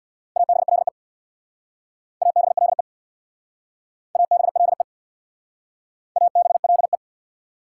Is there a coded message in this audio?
Yes: Morse code from 0.4 to 7.0 s, heard as a keyed tone, about 66 wpm, 690 Hz. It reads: ABLE ABLE ABLE ABLE